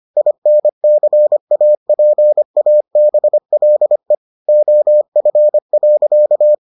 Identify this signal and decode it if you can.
Morse code, heard as a keyed tone, about 25 wpm, 600 Hz. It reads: INCAPABLE OF.